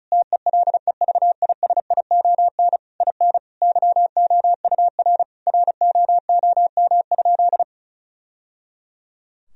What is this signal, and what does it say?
Morse code, heard as a keyed tone, about 36 wpm, 700 Hz. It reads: TELEVISION IN YOUR ROOM?
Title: Sentences from Top 1000 Words - Review Part 4 - 35wpm